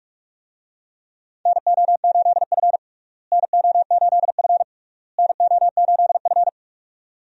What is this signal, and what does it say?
Morse code, heard as a keyed tone, about 45 wpm, 700 Hz. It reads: NO8F NO8F NO8F
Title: USA Call Signs - 4 Characters - Repeated 3x - 45wpm